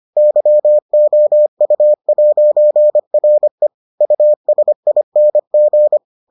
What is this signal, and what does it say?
Morse code, heard as a keyed tone, about 25 wpm, 600 Hz. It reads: YOU'RE USING